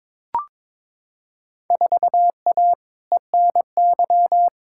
Morse code, heard as a keyed tone, about 22 wpm, 700 Hz. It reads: 4A ENY
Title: Field Day Exchanges - 22wpm